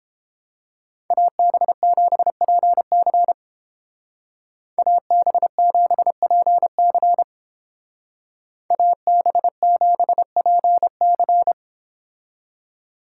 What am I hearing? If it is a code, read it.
Morse code, heard as a keyed tone, about 29 wpm, 700 Hz. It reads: AB7PC AB7PC AB7PC